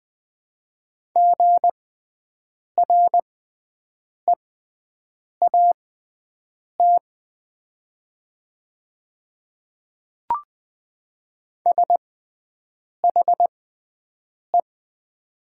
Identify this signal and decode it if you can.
Morse code, heard as a keyed tone, about 20 wpm, spaced as for 7 wpm, 700 Hz. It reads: GREAT SHE
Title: 6x Character Spacing - Top 300 Words - 20wpm